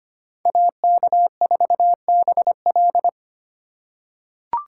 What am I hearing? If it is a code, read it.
Morse code, heard as a keyed tone, about 25 wpm, 700 Hz. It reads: AK4BL
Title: USA Call Signs - 2x Character Spacing - 25wpm